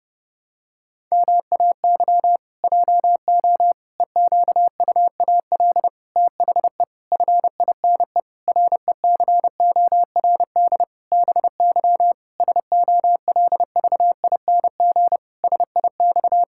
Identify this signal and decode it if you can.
Morse code, heard as a keyed tone, about 30 wpm, 700 Hz. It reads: MAY JO EQUAL THE FINE RECORD BY SOLVING SIX
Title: Pangram Challenge v2 - 30wpm